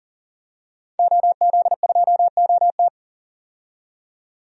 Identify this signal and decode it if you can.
Morse code, heard as a keyed tone, about 39 wpm, 700 Hz. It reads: OZ2OT
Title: DXCC-100 Call Signs - 5 Characters - 40wpm